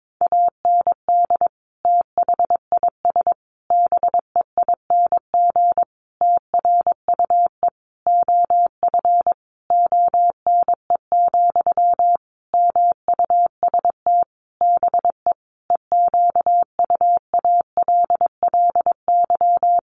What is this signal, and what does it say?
Morse code, heard as a keyed tone, about 22 wpm, 700 Hz. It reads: AND THIS BEING TRUE OF ONE, MUST BE EQUALLY
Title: Sentences from Top 1400 Words 22wpm